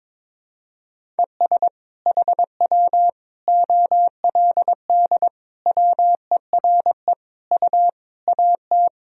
Morse code, heard as a keyed tone, about 22 wpm, 700 Hz. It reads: ES HW OLD WERE U AT
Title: Sentences from Top 200 Words and Abbreviations 22wpm